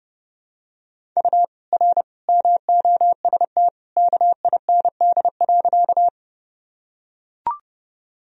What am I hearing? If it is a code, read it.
Morse code, heard as a keyed tone, about 30 wpm, 700 Hz. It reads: U R MOST KIND.
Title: Sentences from Top 200 Words and Abbreviations 30wpm